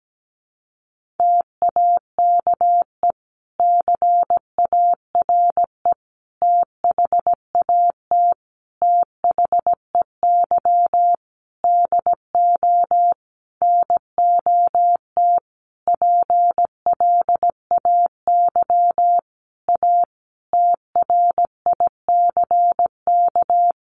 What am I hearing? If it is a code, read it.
Morse code, heard as a keyed tone, about 17 wpm, 700 Hz. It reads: TAKE CARE THAT THEY DO NOT PLAY A TRICK
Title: Sentences from Top 1500 Words 17wpm